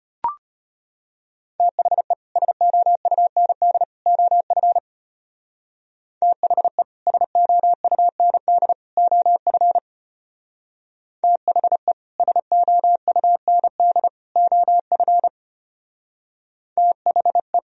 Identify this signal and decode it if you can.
Morse code, heard as a keyed tone, about 33 wpm, 700 Hz. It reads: THE SOUND OF THE SOUND OF THE SOUND OF THE